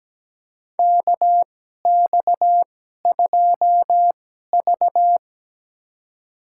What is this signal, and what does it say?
Morse code, heard as a keyed tone, about 17 wpm, spaced as for 12 wpm, 700 Hz. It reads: KX2V
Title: USA Call Signs - 4 Characters - 2x Character Spacing - 17wpm